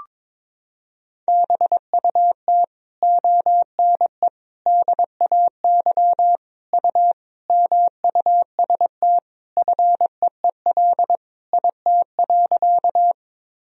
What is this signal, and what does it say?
Morse code, heard as a keyed tone, about 22 wpm, 700 Hz. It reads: BUT ONE DAY U MUST FEEL IT.
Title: Sentences from Top 400 Words and Abbreviations 22wpm